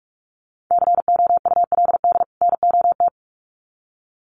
Morse code, heard as a keyed tone, about 46 wpm, 700 Hz. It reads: COULD NOT
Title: Sets of 2 Words 45wpm